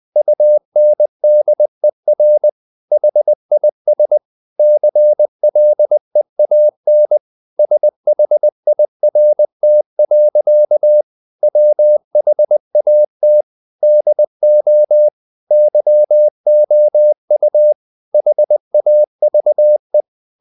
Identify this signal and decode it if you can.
Morse code, heard as a keyed tone, about 20 wpm, 600 Hz. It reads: UNDER HIS CLEAN SHIRT. WHAT DO YOU HAVE
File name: the old man and the sea / the old man and the sea part 1 20wpm